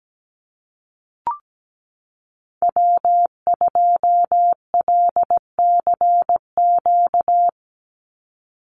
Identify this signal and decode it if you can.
Morse code, heard as a keyed tone, about 17 wpm, 700 Hz. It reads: W2LCQ
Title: Call Signs Repeated 2x - 17wpm